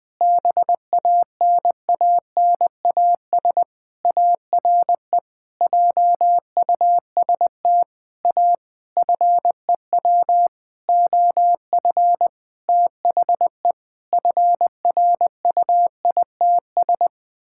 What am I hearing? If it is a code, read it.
Morse code, heard as a keyed tone, about 21 wpm, 700 Hz. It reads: BANANAS ARE JUST A FEW OF THE FRUITS
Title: Sentences with 6-Word Phrases - 20wpm